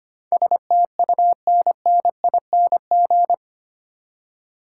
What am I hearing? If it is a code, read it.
Morse code, heard as a keyed tone, about 25 wpm, 700 Hz. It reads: STUNNING
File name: Eight Letter Words - 25wpm